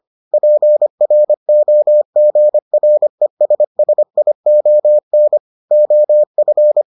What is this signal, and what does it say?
Morse code, heard as a keyed tone, about 25 wpm, 600 Hz. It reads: PROGRESSION OF